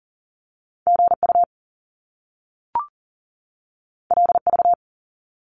Morse code, heard as a keyed tone, about 41 wpm, 700 Hz. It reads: GU LV
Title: Top Two Letter Combinations in Words 40wpm